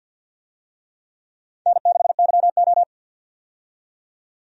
Morse code, heard as a keyed tone, about 52 wpm, 700 Hz. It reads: N6YO
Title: USA Call Signs - 4 Characters - 50wpm